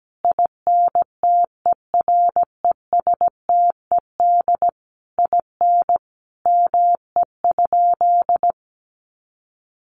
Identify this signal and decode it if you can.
Morse code, heard as a keyed tone, about 17 wpm, 700 Hz. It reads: INTERESTED IN ME?
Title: Sentences from Top 900 Words 17wpm